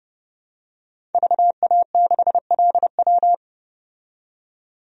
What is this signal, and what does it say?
Morse code, heard as a keyed tone, about 30 wpm, 700 Hz. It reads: VA6LW
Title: DXCC-100 Call Signs - 5 Characters - 30wpm